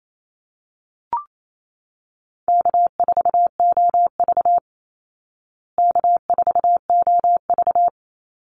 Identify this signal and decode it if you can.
Morse code, heard as a keyed tone, about 28 wpm, 700 Hz. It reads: K4OV K4OV
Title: Call Signs Repeated 2x - 28wpm